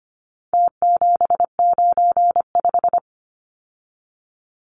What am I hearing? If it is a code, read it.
Morse code, heard as a keyed tone, about 25 wpm, 700 Hz. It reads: T795
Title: Numbers - Contest Style - 4-Digit 25wpm